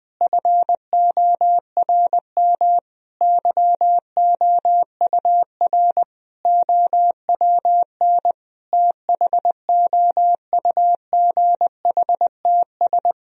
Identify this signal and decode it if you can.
Morse code, heard as a keyed tone, about 21 wpm, 700 Hz. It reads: FORM YOUR OWN THOUGHTS
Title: Sentences from Top 1000 Words - Review Part 1 - 20wpm